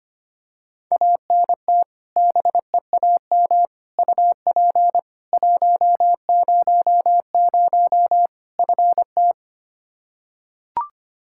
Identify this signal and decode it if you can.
Morse code, heard as a keyed tone, about 25 wpm, 700 Hz. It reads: ANT BEAM UP 100 FT
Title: K1USN SST QSO Practice 25wpm